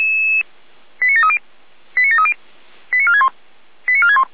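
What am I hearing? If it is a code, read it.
Five-tone selective call (ZVEI), heard as quick runs of five tones, tones of about 70 ms. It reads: 89730 89730 89461 89461
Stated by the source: Five-tone ZVEI selective call received by SDR, fire brigade Feuerwehr Orlamünde.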